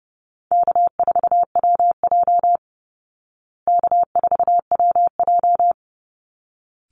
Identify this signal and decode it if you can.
Morse code, heard as a keyed tone, about 30 wpm, 700 Hz. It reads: K4WJ K4WJ
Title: Easy Call Signs Repeated 2x - 30wpm